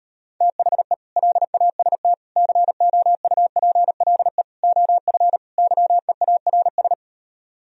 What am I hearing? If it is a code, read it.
Morse code, heard as a keyed tone, about 38 wpm, 700 Hz. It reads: THE LAST COUPLE OF YEARS